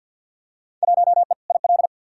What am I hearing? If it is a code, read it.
Morse code, heard as a keyed tone, about 53 wpm, 700 Hz. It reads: 1E IL
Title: Field Day Exchanges - 50wpm